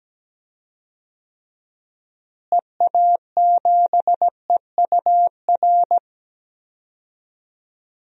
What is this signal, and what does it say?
Morse code, heard as a keyed tone, about 17 wpm, 700 Hz. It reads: EA7EUR
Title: DXCC-100 Call Signs - 6 Characters - Repeated 2x - 17wpm